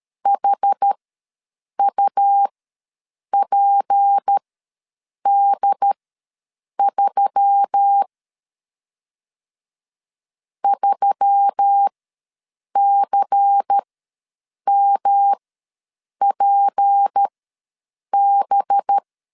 Morse code, heard as a keyed tone, about 13 wpm, spaced as for 7 wpm, 800 Hz. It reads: HUPD3 3CMPB
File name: LEZIONI / 6 PKX0 / 6 P ws 12wpm 3117